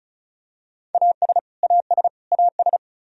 Morse code, heard as a keyed tone, about 34 wpm, 700 Hz. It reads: AS AS AS
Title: Top 100 Words Repeated 35wpm